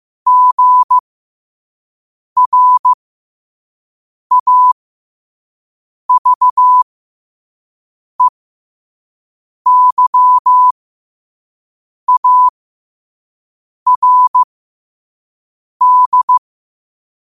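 Morse code, heard as a keyed tone, about 15 wpm, spaced as for 5 wpm, 1000 Hz. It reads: GRAVEYARD
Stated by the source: Morse code practice at 5 wpm.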